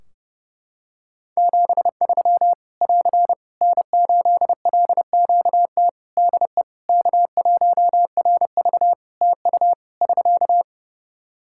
Morse code, heard as a keyed tone, about 32 wpm, 700 Hz. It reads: 73 <AR> N8LQT DE K1RV TU <SK>
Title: K1USN SST QSO Practice 30wpm